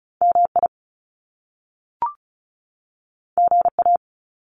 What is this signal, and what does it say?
Morse code, heard as a keyed tone, about 36 wpm, 700 Hz. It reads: MI GA
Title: US States 35wpm